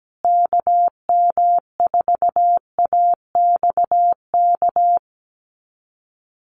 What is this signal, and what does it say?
Morse code, heard as a keyed tone, about 17 wpm, 700 Hz. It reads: KM4AXK
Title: USA Call Signs Repeated 3x 17wpm